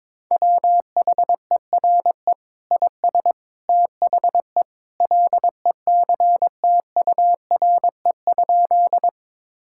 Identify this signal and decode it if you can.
Morse code, heard as a keyed tone, about 22 wpm, 700 Hz. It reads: WHERE IS THE LECTURE?
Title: Sentences from Top 1600 Words 22wpm